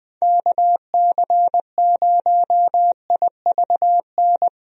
Morse code, heard as a keyed tone, about 20 wpm, 700 Hz. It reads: KC0IVN
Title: USA Call Signs - 6 Characters - Repeated 2x - 20wpm